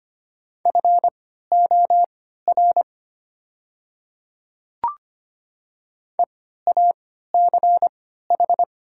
Morse code, heard as a keyed tone, about 25 wpm, spaced as for 14 wpm, 700 Hz. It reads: FOR EACH
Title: 3x Character Spacing - Top 100 Words - 25wpm